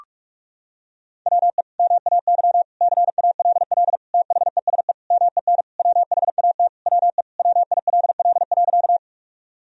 Morse code, heard as a keyed tone, about 46 wpm, 700 Hz. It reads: WE MAY CALL THESE MEN WHAT WE WILL.
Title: Sentences from Top 1000 Words - Review Part 1 - 45wpm